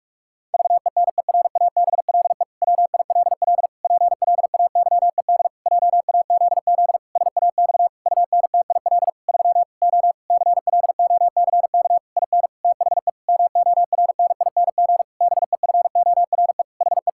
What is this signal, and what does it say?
Morse code, heard as a keyed tone, about 46 wpm, 700 Hz. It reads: VENERABLE WILL PLAYED JAZZ SAX UNTIL 3 O CLOCK IN THE MORNING BEFORE HE
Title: Pangram Challenge v2 - 45wpm